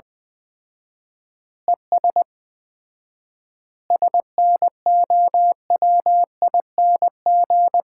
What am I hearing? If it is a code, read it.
Morse code, heard as a keyed tone, about 21 wpm, 700 Hz. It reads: ES SNOWING
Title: Common QSO Phrases - 4x Word-Spacing - 20wpm